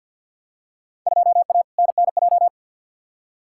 Morse code, heard as a keyed tone, about 53 wpm, 700 Hz. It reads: 2A NNJ
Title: Field Day Exchanges - 50wpm